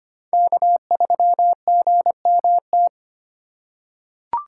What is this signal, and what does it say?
Morse code, heard as a keyed tone, about 25 wpm, 700 Hz. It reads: K3GMT
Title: K1USN SST QSO Practice - 25wpm